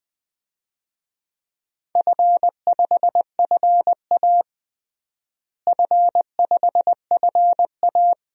Morse code, heard as a keyed tone, about 21 wpm, 700 Hz. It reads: F5FA F5FA
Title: DXCC-100 Call Signs - 4 Characters - Repeated 2x - 20wpm